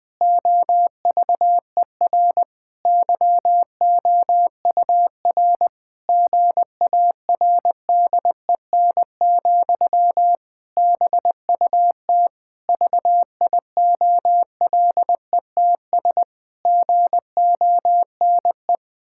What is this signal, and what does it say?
Morse code, heard as a keyed tone, about 20 wpm, 700 Hz. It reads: OVER YOUR GARDEN, BUT VIOLETS GONE
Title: Pangram Challenge v2 - 20wpm